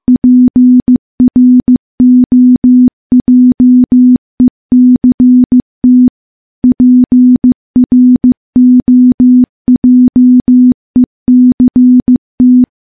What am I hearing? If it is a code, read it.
Morse code, heard as a keyed tone, about 15 wpm, 250 Hz. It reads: PROJECT PROJECT
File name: morse / morse-training-15WPM